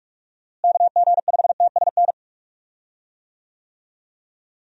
Morse code, heard as a keyed tone, about 45 wpm, 700 Hz. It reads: KG5TSN
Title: USA Call Signs - 6 Characters - Repeated 2x - 45wpm